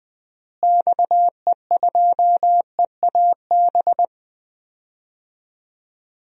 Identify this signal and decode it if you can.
Morse code, heard as a keyed tone, about 21 wpm, 700 Hz. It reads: XE2EAB
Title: DXCC-100 Call Signs - 6 Characters - Repeated 3x - 20wpm